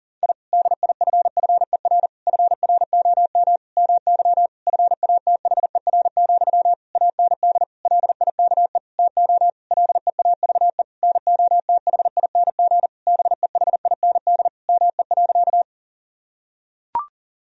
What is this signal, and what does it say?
Morse code, heard as a keyed tone, about 37 wpm, 700 Hz. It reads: I DIFFER FROM MY FATHER, AND LIKE TO LEAVE NOTHING BEHIND ME.
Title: Sentences from Top 1000 Words - Review Part 1 - 40wpm